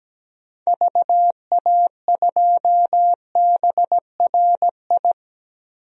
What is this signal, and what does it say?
Morse code, heard as a keyed tone, about 17 wpm, 700 Hz. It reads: VA2BRI
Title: DXCC-100 Call Signs - 6 Characters - 17wpm